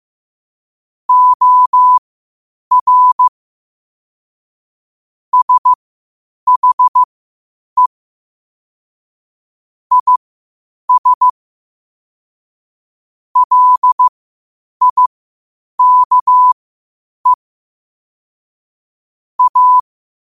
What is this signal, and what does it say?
Morse code, heard as a keyed tone, about 15 wpm, spaced as for 8 wpm, 1000 Hz. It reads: OR SHE IS LIKE A
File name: qotdfiles / 8wpm